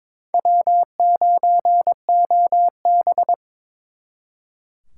Morse code, heard as a keyed tone, about 26 wpm, 700 Hz. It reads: W9OB
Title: Speed Racing - USA 1x2 Call Signs 22wpm